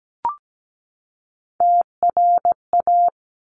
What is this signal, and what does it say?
Morse code, heard as a keyed tone, about 17 wpm, 700 Hz. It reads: TRA
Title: Top Three Letter Combinations in Words 17wpm